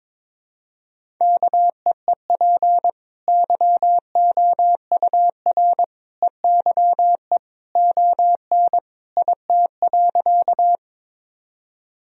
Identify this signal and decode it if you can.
Morse code, heard as a keyed tone, about 22 wpm, 700 Hz. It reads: KEEP YOUR EYE ON IT.
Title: Sentences from Top 1000 Words - Review Part 1 - 22wpm